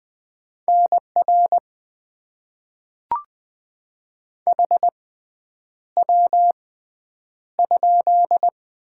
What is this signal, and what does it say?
Morse code, heard as a keyed tone, about 20 wpm, spaced as for 9 wpm, 700 Hz. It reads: NR HW?
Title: Top 90 QSO Elements - 6x Character Spacing - 20wpm